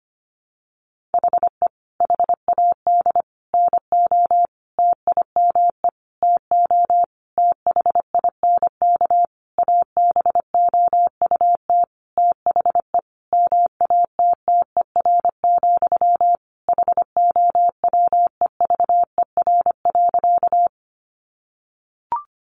Morse code, heard as a keyed tone, about 25 wpm, 700 Hz. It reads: HE HAD NO TIME TO THINK ABOUT THE MATTER, HOWEVER.
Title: Sentences from Top 700 Words 25wpm